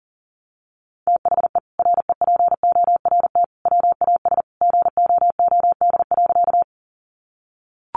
Morse code, heard as a keyed tone, about 40 wpm, 700 Hz. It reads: THE REPORT WAS GOOD.
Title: Sentences from Top 900 Words 40wpm